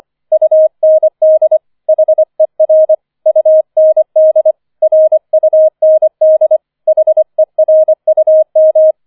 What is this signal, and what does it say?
Morse code, heard as a keyed tone, about 25 wpm, 630 Hz. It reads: UND HER UND RUND HERUM
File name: 005 Fünfter Streich 25wpm